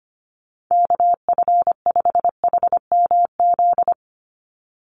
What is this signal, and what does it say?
Morse code, heard as a keyed tone, about 25 wpm, 690 Hz. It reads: KF5HMZ
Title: USA Call Signs 25wpm